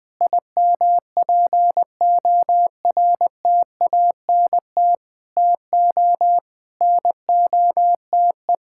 Morse code, heard as a keyed tone, about 20 wpm, 700 Hz. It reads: IMPORTANT TO NOTE
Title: Sentences with 5-Word Phrases - 20wpm